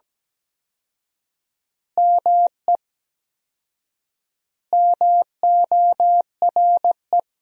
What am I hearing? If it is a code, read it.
Morse code, heard as a keyed tone, about 17 wpm, 700 Hz. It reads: ME MORE
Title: 4x Word-Spacing - Sentences from Top 200 Words - 17wpm